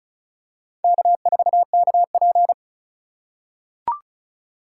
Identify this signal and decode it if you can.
Morse code, heard as a keyed tone, about 36 wpm, 700 Hz. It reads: K4KP
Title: USA Call Signs - 4 Characters - 4x Character Spacing - 35wpm